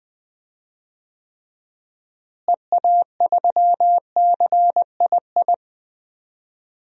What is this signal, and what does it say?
Morse code, heard as a keyed tone, about 20 wpm, 700 Hz. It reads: EA3CII
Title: DXCC-100 Call Signs - 6 Characters - Repeated 2x - 20wpm